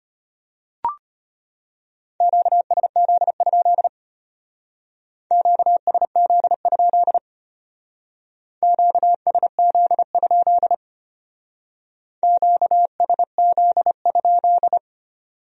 Morse code, heard as a keyed tone, about 32 wpm, 700 Hz. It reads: QSZ? QSZ? QSZ? QSZ?